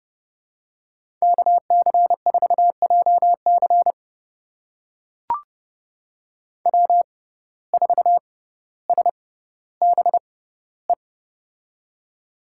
Morse code, heard as a keyed tone, about 30 wpm, spaced as for 15 wpm, 700 Hz. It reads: KC4JC W4SBE
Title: USA Call Signs - 5 Characters - 6x Character Spacing - 30wpm